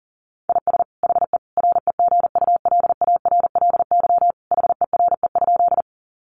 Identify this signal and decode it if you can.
Morse code, heard as a keyed tone, about 40 wpm, 700 Hz. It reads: IS HE REGULARLY HERE?
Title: Sentences from Top 1400 Words 40wpm